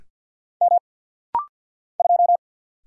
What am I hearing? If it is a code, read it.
Morse code, heard as a keyed tone, about 50 wpm, 700 Hz. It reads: M 2
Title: Single Letter-Number Alt - Rapid-Fire 50wpm